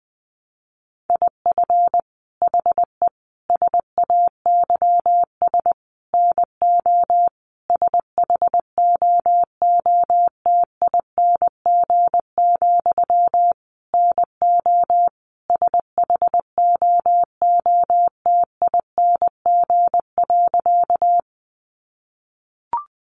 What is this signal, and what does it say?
Morse code, heard as a keyed tone, about 20 wpm, 700 Hz. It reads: IF HE SAYS NO SHOOTING, NO SHOOTING.
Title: A Sound Of Thunder 20wpm